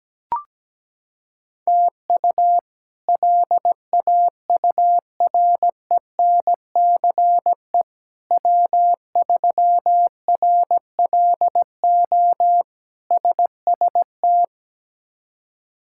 Morse code, heard as a keyed tone, about 17 wpm, 700 Hz. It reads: TU LAURENCE W3RLO SST
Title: K1USN SST QSO Practice - 17wpm